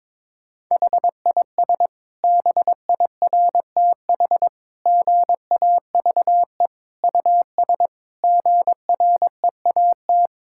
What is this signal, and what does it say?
Morse code, heard as a keyed tone, about 22 wpm, 700 Hz. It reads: HIS BIRTH GAVE US GREAT
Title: Sentences from Top 1000 Words - Review Part 5 - 22wpm